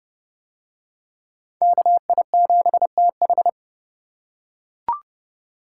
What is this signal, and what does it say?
Morse code, heard as a keyed tone, about 29 wpm, 700 Hz. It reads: KI7TH